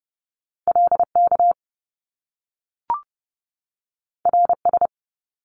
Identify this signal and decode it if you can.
Morse code, heard as a keyed tone, about 30 wpm, 700 Hz. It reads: LK RS
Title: Top Two Letter Combinations in Words 30wpm